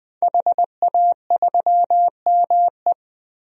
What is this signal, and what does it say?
Morse code, heard as a keyed tone, about 20 wpm, 700 Hz. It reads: HA3ME
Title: DXCC-100 Call Signs - 5 Characters - 20wpm